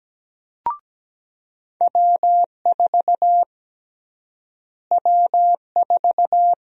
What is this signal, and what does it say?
Morse code, heard as a keyed tone, about 17 wpm, 700 Hz. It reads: W4 W4
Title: USA Call Sign Prefixes - Repeated 2x - 17wpm